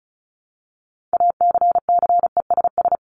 Morse code, heard as a keyed tone, about 34 wpm, 700 Hz. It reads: ACCESS
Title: Top 501-750 Words 35wpm